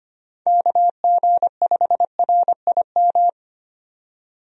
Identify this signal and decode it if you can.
Morse code, heard as a keyed tone, about 25 wpm, 700 Hz. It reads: KG5RIM
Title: USA Call Signs - 6 Characters - 25wpm